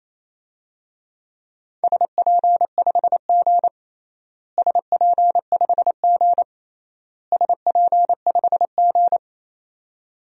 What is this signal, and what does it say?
Morse code, heard as a keyed tone, about 28 wpm, 700 Hz. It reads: SP5G SP5G SP5G
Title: DXCC-100 Call Signs - 4 Characters - Repeated 3x - 28wpm